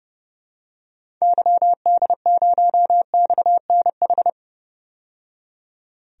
Morse code, heard as a keyed tone, about 30 wpm, 700 Hz. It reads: YD0XNH
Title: DXCC-100 Call Signs - 6 Characters - 30wpm